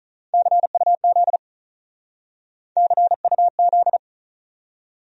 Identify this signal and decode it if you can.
Morse code, heard as a keyed tone, about 45 wpm, 700 Hz. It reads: CUZ CUZ